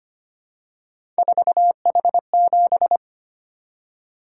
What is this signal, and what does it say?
Morse code, heard as a keyed tone, about 25 wpm, 700 Hz. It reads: 4H7
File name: Three Letter-Number Combinations - 25wpm